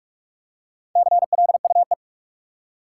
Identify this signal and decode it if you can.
Morse code, heard as a keyed tone, about 46 wpm, 700 Hz. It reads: CLUE
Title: Four Letter Words - 45wpm